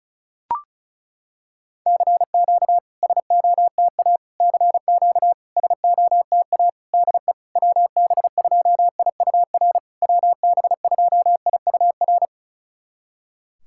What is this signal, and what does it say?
Morse code, heard as a keyed tone, about 36 wpm, 700 Hz. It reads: CQ SOTA CQ SOTA DE WB2IUR WB2IUR
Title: SOTA QSO Practice 35wpm